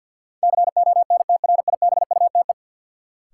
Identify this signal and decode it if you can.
Morse code, heard as a keyed tone, about 52 wpm, 700 Hz. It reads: CONTRIBUTE
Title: Sentences from Top 1400 Words - New Word Drill Alt - 50wpm